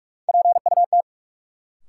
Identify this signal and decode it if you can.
Morse code, heard as a keyed tone, about 42 wpm, 700 Hz. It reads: PUT
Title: Sentences from Top 200 Words - New Word Drill - 45wpm